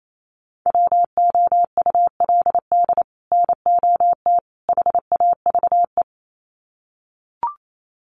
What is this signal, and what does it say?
Morse code, heard as a keyed tone, about 28 wpm, 700 Hz. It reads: WOULD NOT HAVE
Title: Sets of 3 Words 28wpm